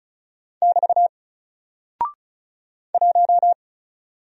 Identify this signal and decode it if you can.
Morse code, heard as a keyed tone, about 36 wpm, 700 Hz. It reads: <BT> 1
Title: Single Character (Letter:Number:Punctuation:Prosign) 35wpm